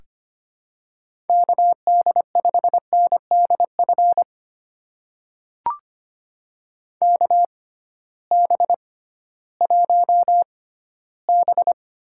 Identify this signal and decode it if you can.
Morse code, heard as a keyed tone, about 25 wpm, spaced as for 15 wpm, 700 Hz. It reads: KD5NDF KB1B
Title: USA Call Signs - 6 Characters - 6x Character Spacing - 25wpm